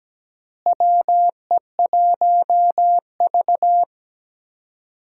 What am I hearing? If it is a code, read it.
Morse code, heard as a keyed tone, about 17 wpm, 700 Hz. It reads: WE1V
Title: USA Call Signs - 4 Characters - Repeated 3x - 17wpm